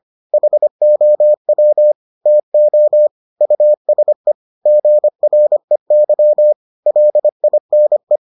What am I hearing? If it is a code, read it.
Morse code, heard as a keyed tone, about 25 wpm, 600 Hz. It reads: HOW TO USE GREY LINE